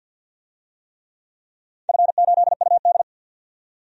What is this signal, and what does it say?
Morse code, heard as a keyed tone, about 46 wpm, 700 Hz. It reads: F8UD
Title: DXCC-100 Call Signs - 4 Characters - 50wpm